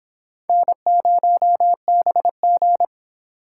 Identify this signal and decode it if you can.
Morse code, heard as a keyed tone, about 26 wpm, 700 Hz. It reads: N0BG